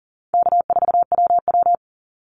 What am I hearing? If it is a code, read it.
Morse code, heard as a keyed tone, about 41 wpm, 700 Hz. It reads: K4WW
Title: Easy Call Signs - 40wpm